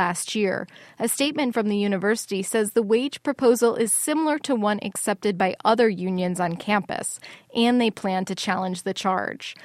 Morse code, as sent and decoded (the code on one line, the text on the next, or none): none